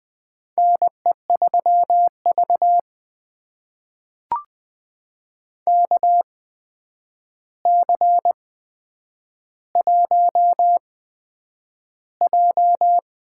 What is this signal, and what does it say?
Morse code, heard as a keyed tone, about 20 wpm, spaced as for 9 wpm, 700 Hz. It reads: NE3V KC1J
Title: USA Call Signs - 4 Characters - 8x Character Spacing - 20wpm